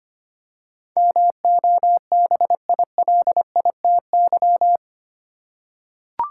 Morse code, heard as a keyed tone, about 25 wpm, 700 Hz. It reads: MOBILITY
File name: Eight Letter Words - 25wpm